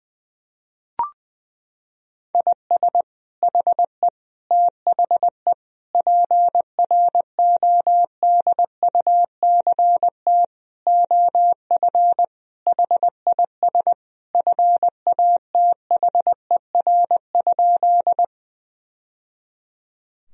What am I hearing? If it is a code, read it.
Morse code, heard as a keyed tone, about 20 wpm, 700 Hz. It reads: IS HE THE PRODUCT OF HIS FATHER?
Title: Sentences from Top 1000 Words - Review Part 2 - 20wpm